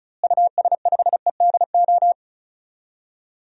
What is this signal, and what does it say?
Morse code, heard as a keyed tone, about 36 wpm, 700 Hz. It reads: US5EDO
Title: DXCC-100 Call Signs - 6 Characters - Repeated 2x - 35wpm